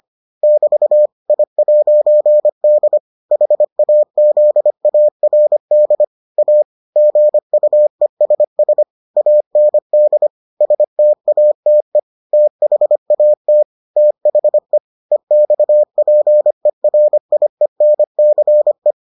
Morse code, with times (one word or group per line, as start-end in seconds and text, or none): 0.4-1.1 <BT>
1.3-3.0 I'D
3.3-6.0 HAZARD
6.4-6.6 A
7.0-8.8 GUESS
9.2-10.3 AND
10.6-12.0 STATE
12.3-13.6 THAT
14.0-14.8 THE
15.1-18.9 EXPERIENCE